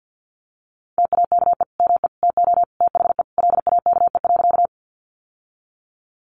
Morse code, heard as a keyed tone, about 50 wpm, 700 Hz. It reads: TAKE ME TO THE LAKE.
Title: Sentences from Top 1100 Words 50wpm